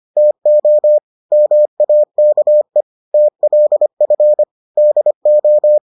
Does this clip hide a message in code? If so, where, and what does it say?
Yes: Morse code from 0.2 to 5.8 s, heard as a keyed tone, about 25 wpm, 600 Hz. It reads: TO MAKE TLF DO